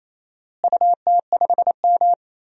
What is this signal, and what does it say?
Morse code, heard as a keyed tone, about 28 wpm, 700 Hz. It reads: UT5M
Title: DXCC-100 Call Signs - 4 Characters - Repeated 3x - 28wpm